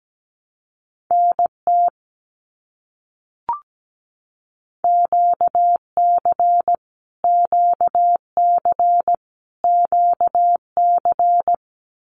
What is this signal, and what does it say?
Morse code, heard as a keyed tone, about 17 wpm, 700 Hz. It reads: NT QC QC QC
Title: Canadian Provinces and Territories - Repeated 3x - 17wpm